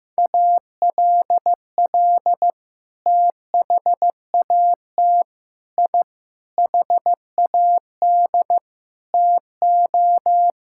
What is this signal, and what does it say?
Morse code, heard as a keyed tone, about 15 wpm, 700 Hz. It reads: ALL THAT I HAD TO
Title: Sentences from Top 1000 Words - Review Part 6 - 15wpm